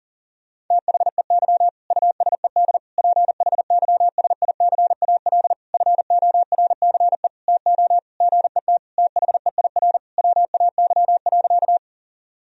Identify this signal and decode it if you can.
Morse code, heard as a keyed tone, about 40 wpm, 700 Hz. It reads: THEY USED PHYSICAL FORCE TO GET THEIR WAY.
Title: Sentences from Top 1000 Words - Review Part 4 - 40wpm